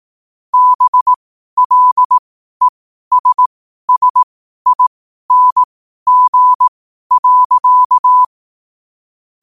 Morse code, heard as a keyed tone, about 18 wpm, spaced as for 12 wpm, 1000 Hz. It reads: BLESSING.
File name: qotdfiles / 12wpm